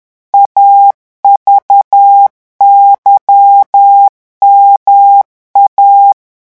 Morse code, heard as a keyed tone, about 10 wpm, 800 Hz. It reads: AVYMA
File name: morse mp3 / revision2-vitesse 10 mots